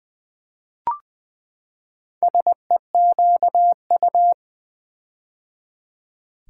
Morse code, heard as a keyed tone, about 21 wpm, 700 Hz. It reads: SEQU
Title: Build A Word - 4 Letters At A Time - 8-Letter Words - 20wpm